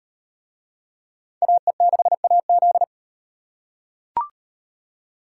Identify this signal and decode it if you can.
Morse code, heard as a keyed tone, about 38 wpm, 700 Hz. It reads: AE6AZ